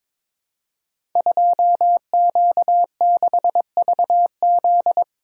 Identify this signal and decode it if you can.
Morse code, heard as a keyed tone, about 22 wpm, 700 Hz. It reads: 2Q6VZ
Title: Code Groups - 5 Letter:Numbers - 22wpm